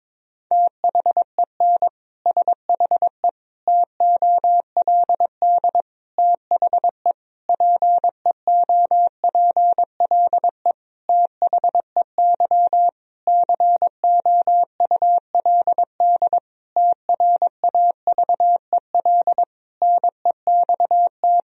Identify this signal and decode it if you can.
Morse code, heard as a keyed tone, about 22 wpm, 700 Hz. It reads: THEN SHE TOLD THE PEOPLE THEY COULD TRAVEL NEXT
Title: Sentences from Top 1000 Words - Review Part 2 - 22wpm